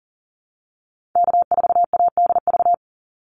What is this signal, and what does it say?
Morse code, heard as a keyed tone, about 40 wpm, 700 Hz. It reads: K4ADV
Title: NA SOTA Activator Callsigns 40wpm